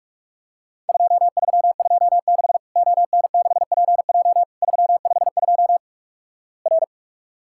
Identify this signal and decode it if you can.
Morse code, heard as a keyed tone, about 48 wpm, 700 Hz. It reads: 2326 ON6PJ 352 R
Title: Worked All Europe DX Contest - QTC Exchange - 45wpm